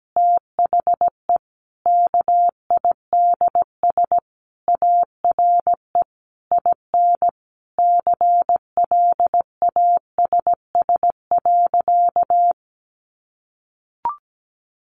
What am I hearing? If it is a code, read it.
Morse code, heard as a keyed tone, about 17 wpm, 700 Hz. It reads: THE KIDS ARE IN CLASS.
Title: Easy Phrases 17wpm